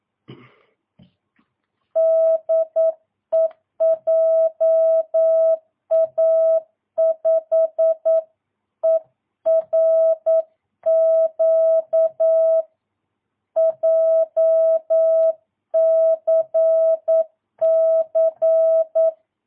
Morse code, heard as a keyed tone, about 9 wpm, 650 Hz. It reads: DE JA5ERQ JCC
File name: jh8cbh / 2022cw